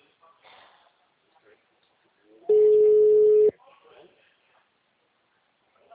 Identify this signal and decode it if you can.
Ringback tone (Europe, 425 Hz), with speech.
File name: out- -2019- - - .4591